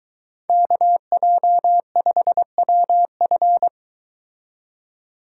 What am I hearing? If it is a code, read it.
Morse code, heard as a keyed tone, about 22 wpm, 700 Hz. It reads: KJ5WF